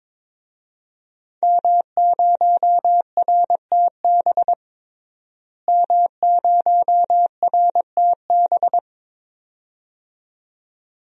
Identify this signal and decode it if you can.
Morse code, heard as a keyed tone, about 22 wpm, 700 Hz. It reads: M0RTB M0RTB
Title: DXCC-100 Call Signs - 5 Characters - Repeated 2x - 22wpm